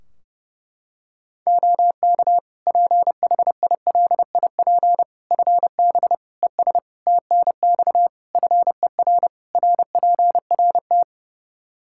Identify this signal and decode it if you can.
Morse code, heard as a keyed tone, about 30 wpm, 700 Hz. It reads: OK PHILIP FB ES TNX FER RPRT
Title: K1USN SST QSO Practice 30wpm